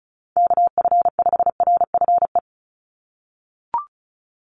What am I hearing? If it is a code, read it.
Morse code, heard as a keyed tone, about 36 wpm, 700 Hz. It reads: KF5RFE
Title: USA Call Signs 35wpm